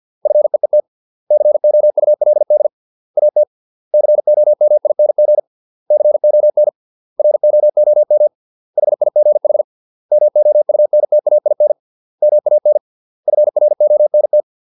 Morse code, heard as a keyed tone, about 50 wpm, 600 Hz. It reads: FEET COULD AT COMING CON ROOM HIGH MOUNTAIN MAN FRONT